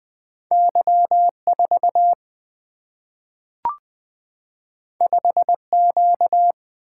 Morse code, heard as a keyed tone, about 20 wpm, 700 Hz. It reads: Y4 5Q
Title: Single-Letter Single-Number 20wpm